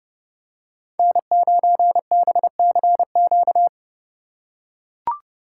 Morse code, heard as a keyed tone, about 31 wpm, 700 Hz. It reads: N9BCQ
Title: USA Call Signs - 5 Characters - 3x Character Spacing - 30wpm